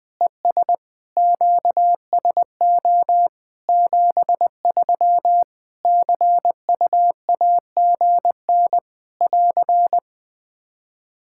Morse code, heard as a keyed tone, about 20 wpm, 700 Hz. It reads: ES QSO 73 CUAGN <AR>
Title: Practice QSO - Exchange 4 - Ending - 20wpm